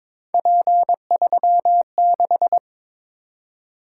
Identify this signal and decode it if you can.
Morse code, heard as a keyed tone, about 22 wpm, 700 Hz. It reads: P36
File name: practice / 22wpm / Three Letter-Number Combinations - 22wpm